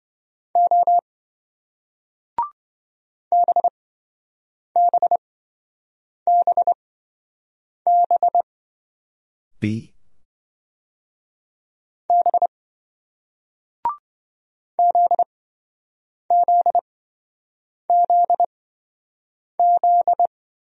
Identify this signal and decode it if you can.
Morse code, heard as a keyed tone, about 27 wpm, spaced as for 7 wpm, 700 Hz. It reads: O BBBB B ZZZZ